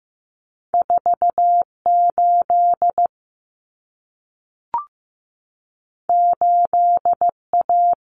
Morse code, heard as a keyed tone, about 15 wpm, 700 Hz. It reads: 4 8 8 A
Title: Common Cut Numbers - 2-Digit 15wpm